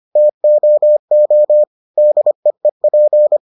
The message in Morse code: TOO DEEP

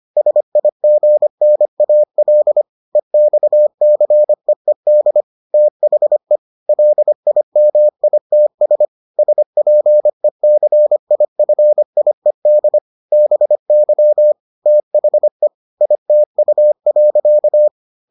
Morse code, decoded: SIGNAL EXCEED THE LIMITS SPECIFIED BY THE ITU.

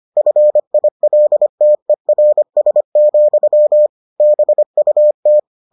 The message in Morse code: FILTERS, BUT

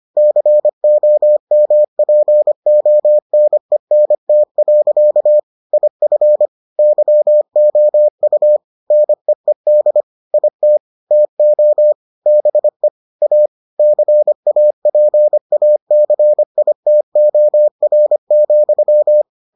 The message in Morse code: COMPONENT. IF YOU NEED IT TO BE A CAPACITOR,

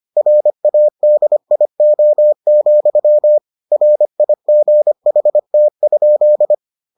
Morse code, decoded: RADIO, RIGHT?

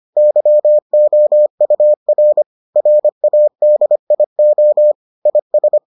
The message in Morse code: YOUR RADIO IS

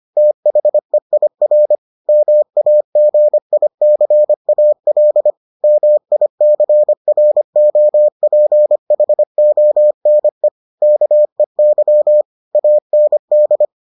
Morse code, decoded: THEIR MAGICAL MICROPHONE KEY AND

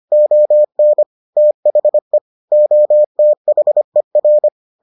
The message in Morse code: ON THE OTHER

